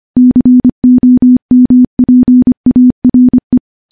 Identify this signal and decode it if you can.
Morse code, heard as a keyed tone, about 25 wpm, 250 Hz. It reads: COMPARE
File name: morse / morse-training-25WPM